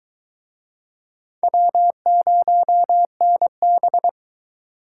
Morse code, heard as a keyed tone, about 20 wpm, 700 Hz. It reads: W0NB